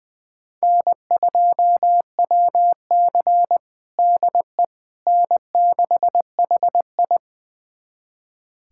Morse code, heard as a keyed tone, about 20 wpm, 700 Hz. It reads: N2WC DE N6HI
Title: USA Callsign DE Callsign - 4 Characters - 20wpm